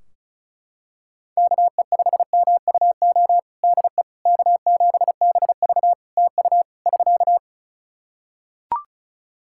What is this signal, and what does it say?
Morse code, heard as a keyed tone, about 36 wpm, 700 Hz. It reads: KE5MUO DE K7BV TU <SK>
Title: K1USN SST QSO Practice 35wpm